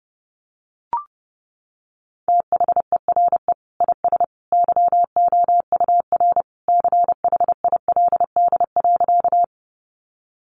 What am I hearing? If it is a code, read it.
Morse code, heard as a keyed tone, about 28 wpm, 700 Hz. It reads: THERE IS YOUR CHILD.
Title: Sentences from Top 800 Words 30wpm